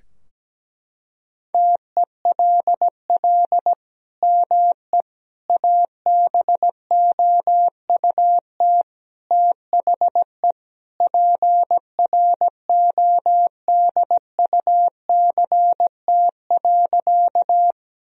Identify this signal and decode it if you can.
Morse code, heard as a keyed tone, about 17 wpm, 700 Hz. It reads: TELL ME ABOUT THE PRODUCT.
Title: Sentences from Top 1000 Words - Review Part 2 - 17wpm